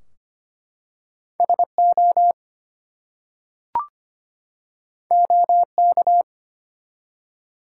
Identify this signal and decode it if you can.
Morse code, heard as a keyed tone, about 25 wpm, 700 Hz. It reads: SO OK
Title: Two Letter Words - 25wpm